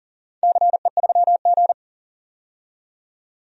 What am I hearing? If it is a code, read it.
Morse code, heard as a keyed tone, about 40 wpm, 700 Hz. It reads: CE3G
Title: DXCC-100 Call Signs - 4 Characters - 40wpm